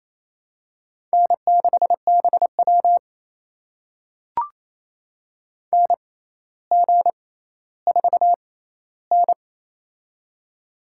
Morse code, heard as a keyed tone, about 28 wpm, spaced as for 14 wpm, 700 Hz. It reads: N6BW NG4N
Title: USA Call Signs - 4 Characters - 6x Character Spacing - 28wpm